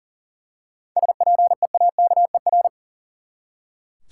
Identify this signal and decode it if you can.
Morse code, heard as a keyed tone, about 40 wpm, 700 Hz. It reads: SPEAKER
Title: Seven Letter Words - 40wpm